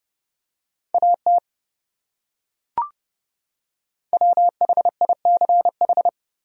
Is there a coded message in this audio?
Yes: Morse code from 0.9 to 6.1 s, heard as a keyed tone, about 29 wpm, 700 Hz. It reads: AT WHICH